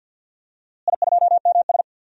Morse code, heard as a keyed tone, about 46 wpm, 700 Hz. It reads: I2MS
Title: DXCC-100 Call Signs - 4 Characters - Repeated 2x - 50wpm